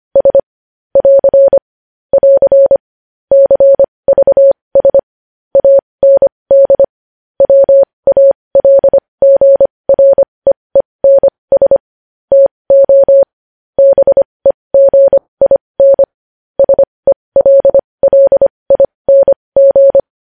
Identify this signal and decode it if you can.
Morse code, heard as a keyed tone, about 25 wpm, spaced as for 20 wpm, 550 Hz. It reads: S <AR> <AR> CVS AND WALGREENS TO BEGIN SELLING
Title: News Headlines at 20